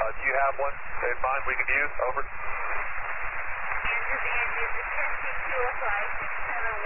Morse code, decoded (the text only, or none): none